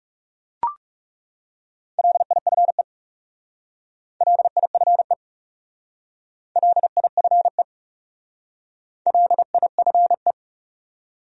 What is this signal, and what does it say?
Morse code, heard as a keyed tone, about 36 wpm, 700 Hz. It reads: LIFE LIFE LIFE LIFE
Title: Speed-Racing Top 500 Words 30wpm